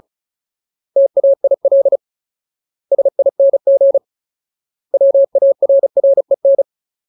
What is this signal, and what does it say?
Morse code, heard as a keyed tone, about 34 wpm, 550 Hz. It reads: TAIL SING WARREN
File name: Ultimate Word List / 35WPM Ultimate